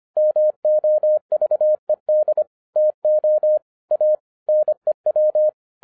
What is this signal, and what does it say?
Morse code, heard as a keyed tone, about 25 wpm, 610 Hz. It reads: MOVED TO A NEW